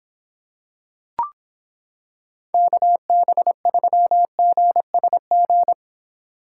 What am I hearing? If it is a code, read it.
Morse code, heard as a keyed tone, about 25 wpm, 700 Hz. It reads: KB3GSG